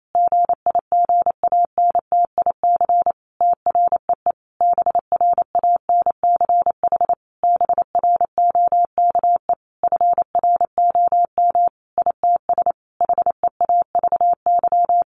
Morse code, heard as a keyed tone, about 28 wpm, 700 Hz. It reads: GIGANTIC TREE BRANCH BROKE FROM ITS HEAVY
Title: A Sound Of Thunder 28wpm